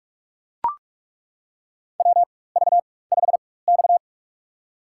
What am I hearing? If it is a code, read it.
Morse code, heard as a keyed tone, about 41 wpm, spaced as for 20 wpm, 700 Hz. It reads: WV5X
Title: USA Call Signs - 4x Character Spacing - 45wpm